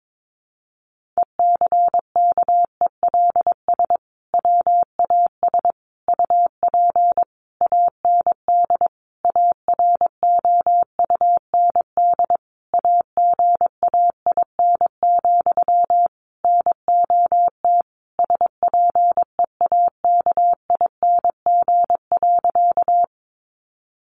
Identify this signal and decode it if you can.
Morse code, heard as a keyed tone, about 22 wpm, 700 Hz. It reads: ECKELS WAS UP AND AROUND AGAIN, NOT SPEAKING.
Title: A Sound Of Thunder 22wpm